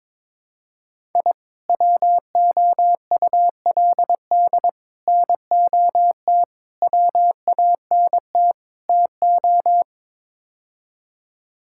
Morse code, heard as a keyed tone, about 22 wpm, 700 Hz. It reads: I WOULD NOT WANT TO
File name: practice / 22wpm / Build A Phrase - 5-Word Phrases - 22wpm